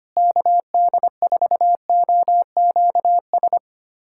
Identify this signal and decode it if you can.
Morse code, heard as a keyed tone, about 25 wpm, 700 Hz. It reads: KD4OQS
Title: USA Callsign DE Callsign - 6 Characters - 25wpm